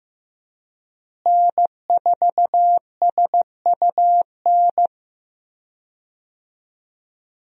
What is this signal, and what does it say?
Morse code, heard as a keyed tone, about 16 wpm, 700 Hz. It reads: N4SUN
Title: USA Call Signs - 5 Characters - Repeated 3x - 15wpm